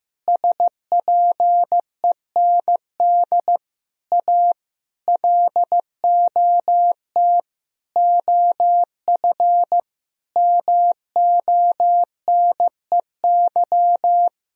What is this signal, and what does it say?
Morse code, heard as a keyed tone, about 15 wpm, 700 Hz. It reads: SPEND A LOT OF MONEY
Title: Sentences from Top 1000 Words - Review Part 4 - 15wpm